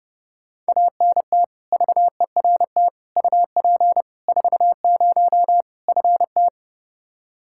Morse code, heard as a keyed tone, about 29 wpm, 700 Hz. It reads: ANT VERT UP 40 FT
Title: Practice QSO - Exchange 2 - Info - 30wpm